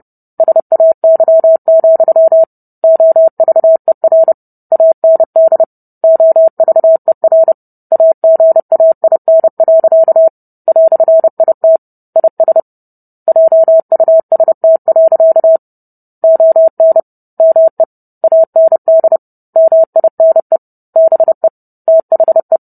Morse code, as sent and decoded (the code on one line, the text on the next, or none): ... .- -.-- --..-- / --- ...- . .-. / .- -. -.. / --- ...- . .-. / .- --. .- .. -. .-.-.- / .-..-. .. - / .. ... / .--- ..- ... - .-.-.- / --- -. / -- . / .- -. -.. / -- .. -. / -... / - .... .
SAY, OVER AND OVER AGAIN. "IT IS JUST. ON ME AND MINE BE THE